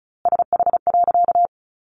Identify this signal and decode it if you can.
Morse code, heard as a keyed tone, about 34 wpm, 700 Hz. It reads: SH.
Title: A Sound Of Thunder 35wpm